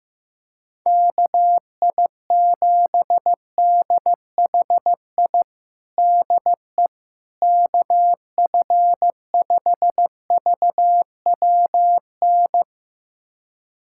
Morse code, heard as a keyed tone, about 16 wpm, 700 Hz. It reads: KI7DHI DE KF5VWN
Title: USA Callsign DE Callsign - 6 Characters - 15wpm